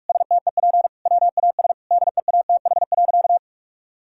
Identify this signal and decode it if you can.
Morse code, heard as a keyed tone, about 46 wpm, 700 Hz. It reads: STEP WAS DEATH.
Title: Sentences from Top 1600 Words 45wpm